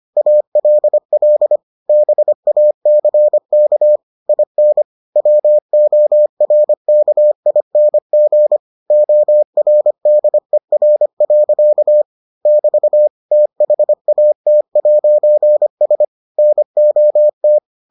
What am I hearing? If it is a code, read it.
Morse code, heard as a keyed tone, about 25 wpm, 600 Hz. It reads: ALL BACK IN WORKING ORDER. <BT> THAT'S NOT